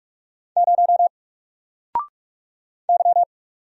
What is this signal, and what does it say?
Morse code, heard as a keyed tone, about 44 wpm, 700 Hz. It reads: 0 Y
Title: Single Character (Letter:Number:Punctuation:Prosign) 45wpm